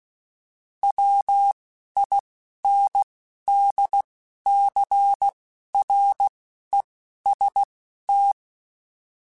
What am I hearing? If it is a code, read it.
Morse code, heard as a keyed tone, about 16 wpm, spaced as for 11 wpm, 800 Hz. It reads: WINDCREST